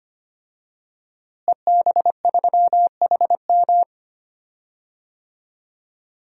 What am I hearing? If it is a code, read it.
Morse code, heard as a keyed tone, about 25 wpm, 700 Hz. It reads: EB3HM
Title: DXCC-100 Call Signs - 5 Characters - Repeated 2x - 25wpm